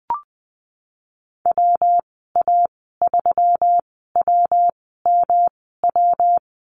Morse code, heard as a keyed tone, about 20 wpm, spaced as for 15 wpm, 700 Hz. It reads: WA3WMW
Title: 122 Lesson 20g - Number 3 - Call Signs 20wpm